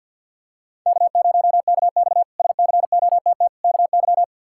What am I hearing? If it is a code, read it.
Morse code, heard as a keyed tone, about 50 wpm, 700 Hz. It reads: K0KX SCOTT KY
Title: K1USN SST QSO Practice - 50wpm